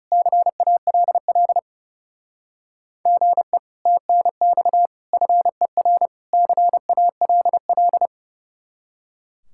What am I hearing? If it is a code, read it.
Morse code, heard as a keyed tone, about 40 wpm, 700 Hz. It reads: CALL GE TNX FER CALL